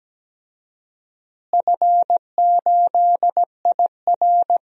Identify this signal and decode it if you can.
Morse code, heard as a keyed tone, about 17 wpm, 700 Hz. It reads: F8IR
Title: DXCC-100 Call Signs - 4 Characters - Repeated 3x - 17wpm